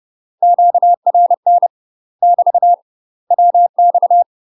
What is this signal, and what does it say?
Morse code, heard as a keyed tone, about 30 wpm, 700 Hz. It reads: QRN <BT> WX
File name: UK MASTER QSOs TRAINING / 30WPM / UK MASTER QSOs 0044